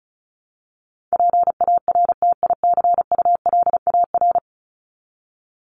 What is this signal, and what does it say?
Morse code, heard as a keyed tone, about 34 wpm, 700 Hz. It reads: PARTICULAR
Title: LY Suffix - Top 100 Words with Base Words 35wpm